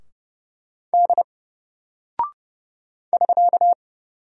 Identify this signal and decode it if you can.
Morse code, heard as a keyed tone, about 30 wpm, 700 Hz. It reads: D <SK>
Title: Single Character (Letter:Number:Punctuation:Prosign) 30wpm